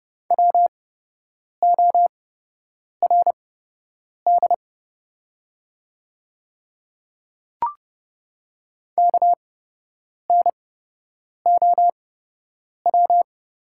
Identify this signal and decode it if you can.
Morse code, heard as a keyed tone, about 31 wpm, spaced as for 8 wpm, 700 Hz. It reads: WORD KNOW
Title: 8x Character Spacing - Top 500 Words - 30wpm